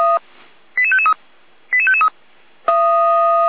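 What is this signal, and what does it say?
Five-tone selective call (ZVEI), heard as quick runs of five tones, tones of about 80 ms. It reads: 88553 88553
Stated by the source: Five-tone ZVEI selective call received on an SDR.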